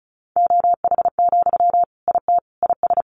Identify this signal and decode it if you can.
Morse code, heard as a keyed tone, about 34 wpm, 700 Hz. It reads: OH, IT IS
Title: Sentences from Top 1000 Words 35wpm